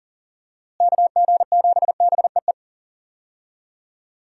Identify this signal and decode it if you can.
Morse code, heard as a keyed tone, about 40 wpm, 700 Hz. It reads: KG7BEE
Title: USA Call Signs - 6 Characters - Repeated 3x - 40wpm